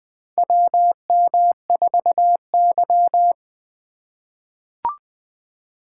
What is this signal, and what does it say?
Morse code, heard as a keyed tone, about 20 wpm, 700 Hz. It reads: WM4Y
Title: USA Call Signs - 4 Characters - 2x Character Spacing - 20wpm